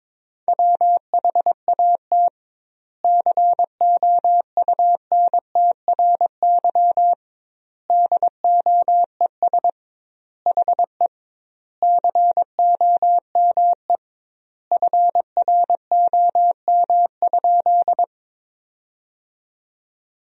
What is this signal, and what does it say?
Morse code, heard as a keyed tone, about 22 wpm, 700 Hz. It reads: WHAT COUNTRY DOES HE COME FROM?
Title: 2x Word-Spacing - Sentences from Top 300 Words - 22wpm